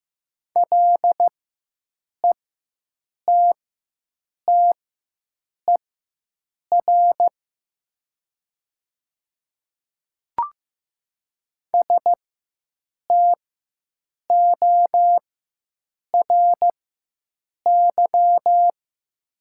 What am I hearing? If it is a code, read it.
Morse code, heard as a keyed tone, about 15 wpm, spaced as for 7 wpm, 700 Hz. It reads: LETTER STORY
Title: 4x Character Spacing - Top 300 Words - 15wpm